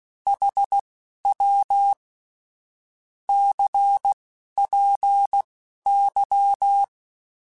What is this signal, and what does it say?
Morse code, heard as a keyed tone, about 16 wpm, spaced as for 11 wpm, 800 Hz. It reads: HW CPY
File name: QSOs / QSOs-mp3s / WD0FBT 1@10